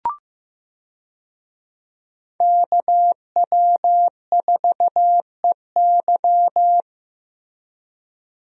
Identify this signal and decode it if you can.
Morse code, heard as a keyed tone, about 15 wpm, 700 Hz. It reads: KW4EY
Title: USA Call Signs - 5 Characters - Repeated 3x - 15wpm